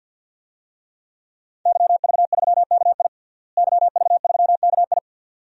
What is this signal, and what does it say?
Morse code, heard as a keyed tone, about 46 wpm, 700 Hz. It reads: YV3KI YV3KI
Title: DXCC-100 Call Signs - 5 Characters - Repeated 2x - 50wpm